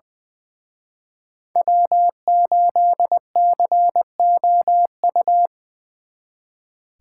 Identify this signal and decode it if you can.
Morse code, heard as a keyed tone, about 25 wpm, 700 Hz. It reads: W8COU
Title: Speed Racing - USA 1x3 Call Signs 20wpm